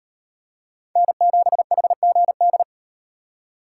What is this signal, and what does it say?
Morse code, heard as a keyed tone, about 38 wpm, 700 Hz. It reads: N7HGD